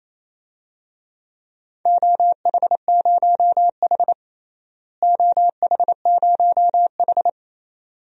Morse code, heard as a keyed tone, about 28 wpm, 700 Hz. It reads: OH0H OH0H
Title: DXCC-100 Call Signs - 4 Characters - Repeated 2x - 28wpm